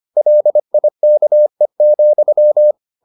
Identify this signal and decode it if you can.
Morse code, heard as a keyed tone, about 25 wpm, 600 Hz. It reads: LIKE,